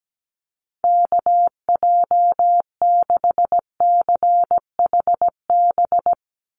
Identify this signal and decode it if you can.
Morse code, heard as a keyed tone, about 17 wpm, 700 Hz. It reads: KJ6CHB
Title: USA Call Signs 17wpm